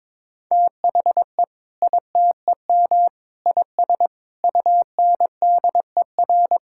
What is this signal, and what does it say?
Morse code, heard as a keyed tone, about 22 wpm, 700 Hz. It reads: THE ITEM IS UNDER
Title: Sentences from Top 1000 Words - Review Part 6 - 22wpm